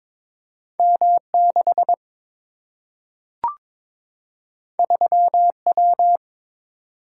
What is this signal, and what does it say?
Morse code, heard as a keyed tone, about 22 wpm, 700 Hz. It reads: M 6 3 W
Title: Single-Letter Single-Number 22wpm